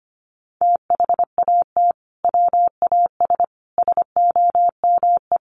Morse code, heard as a keyed tone, about 25 wpm, 700 Hz. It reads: THAT WAS SOME
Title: Sentences from Top 1400 Words 25wpm